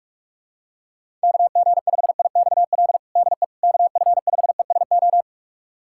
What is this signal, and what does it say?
Morse code, heard as a keyed tone, about 45 wpm, 700 Hz. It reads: KG5IXL DE KF5ESO
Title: USA Callsign DE Callsign - 6 Characters - 45wpm